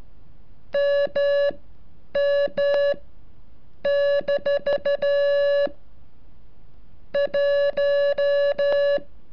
Morse code, heard as a keyed tone, about 13 wpm, spaced as for 8 wpm, 570 Hz. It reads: MM- 1
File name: curso / ejercicio 35